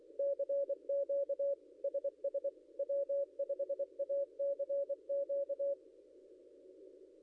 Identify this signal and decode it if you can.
Morse code, heard as a keyed tone, about 24 wpm, 560 Hz. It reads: CQ SS W5ACQ
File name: contests / ss-cw-k5tr-op-n5rz -0800